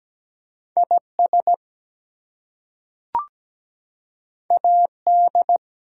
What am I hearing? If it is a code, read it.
Morse code, heard as a keyed tone, about 17 wpm, 700 Hz. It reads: IS AD
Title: Two Letter Words - 17wpm